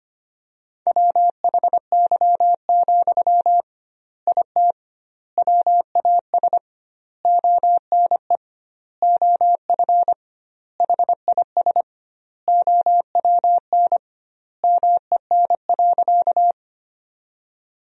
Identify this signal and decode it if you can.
Morse code, heard as a keyed tone, about 25 wpm, 700 Hz. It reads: WHY, IT WAS ONE OF HIS OWN MEN.
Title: 2x Word-Spacing - Sentences from Top 300 Words - 25wpm